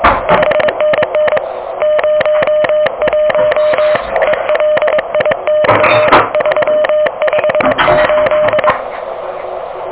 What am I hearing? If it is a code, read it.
Morse code, heard as a keyed tone, about 22 wpm, 590 Hz, with noise. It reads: 5NN 01 ELI/358
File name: OK QRP 2023 / OK QRP 2023